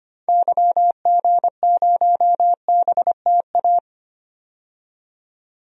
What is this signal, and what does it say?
Morse code, heard as a keyed tone, about 25 wpm, 700 Hz. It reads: YG0BTA
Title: DXCC-100 Call Signs - 6 Characters - 25wpm